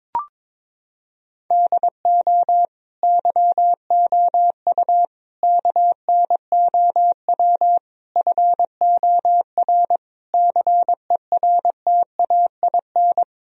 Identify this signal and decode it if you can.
Morse code, heard as a keyed tone, about 22 wpm, 700 Hz. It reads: DO YOU KNOW FOR CERTAIN
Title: Sentences from Top 1000 Words - Review Part 2 - 22wpm